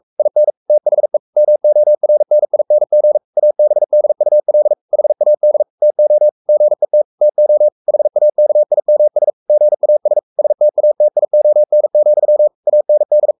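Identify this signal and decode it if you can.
Morse code, heard as a keyed tone, about 44 wpm, 600 Hz. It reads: IN THE MORNING ABDUL HAD TO GET TO HAKIMS GAS STATION, AND
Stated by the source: Morse code at 43 wpm sending Ken Follett's novel Never in English, from CW med Gnister.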